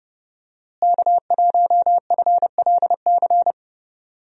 Morse code, heard as a keyed tone, about 31 wpm, 700 Hz. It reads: K1FLC
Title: USA Call Signs - 5 Characters - Repeated 2x - 30wpm